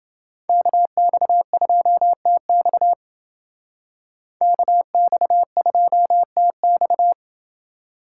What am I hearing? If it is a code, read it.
Morse code, heard as a keyed tone, about 29 wpm, 700 Hz. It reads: KX2TX KX2TX